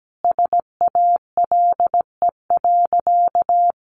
Morse code, heard as a keyed tone, about 17 wpm, 700 Hz. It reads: SALE.
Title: Sentences from Top 1400 Words 17wpm